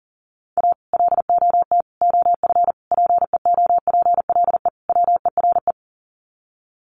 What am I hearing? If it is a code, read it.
Morse code, heard as a keyed tone, about 38 wpm, 700 Hz. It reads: A LOT OF PEOPLE WERE
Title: Sets of 5 Words Encore2 40wpm